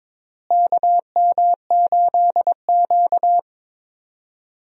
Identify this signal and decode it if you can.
Morse code, heard as a keyed tone, about 22 wpm, 700 Hz. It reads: KM8Q
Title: USA Call Signs - 4 Characters - Repeated 2x - 22wpm